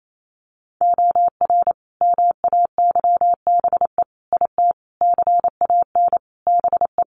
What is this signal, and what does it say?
Morse code, heard as a keyed tone, about 28 wpm, 700 Hz. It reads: OR MAYBE IT CAN BE
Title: A Sound Of Thunder 28wpm